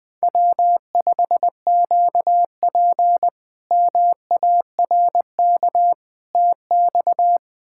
Morse code, heard as a keyed tone, about 21 wpm, 700 Hz. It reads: W5QP MARK TX
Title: K1USN SST QSO Practice - 20wpm